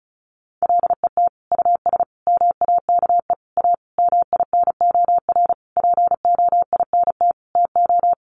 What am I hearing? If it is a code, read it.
Morse code, heard as a keyed tone, about 34 wpm, 700 Hz. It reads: LET US MAKE A MINOR POINT TO
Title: Sentences from Top 1500 Words 35wpm